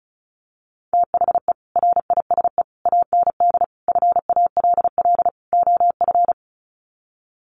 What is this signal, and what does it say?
Morse code, heard as a keyed tone, about 34 wpm, 700 Hz. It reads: THE RISE AND FALL OF
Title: Sets of 5 Words Encore2 35wpm